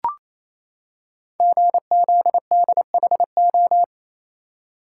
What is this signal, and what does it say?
Morse code, heard as a keyed tone, about 27 wpm, 700 Hz. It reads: GZDHO
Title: Code Groups - 5 Letters - 28wpm